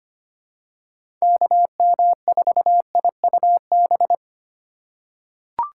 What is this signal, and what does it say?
Morse code, heard as a keyed tone, about 25 wpm, 700 Hz. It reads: KM4IUB
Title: USA Call Signs - 6 Characters - 2x Character Spacing - 25wpm